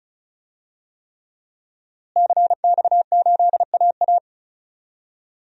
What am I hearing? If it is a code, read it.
Morse code, heard as a keyed tone, about 35 wpm, 700 Hz. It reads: CX8AA
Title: DXCC-100 Call Signs - 5 Characters - Repeated 3x - 35wpm